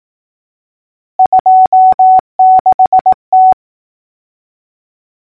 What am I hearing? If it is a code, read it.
Morse code, heard as a keyed tone, about 18 wpm, 750 Hz. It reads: 26T
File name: numbers / V2-morse-clicks